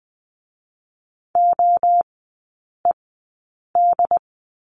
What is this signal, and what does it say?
Morse code, heard as a keyed tone, about 20 wpm, spaced as for 8 wpm, 700 Hz. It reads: OED
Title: 067 Lesson 12e - Letter d - Three Characters 20wpm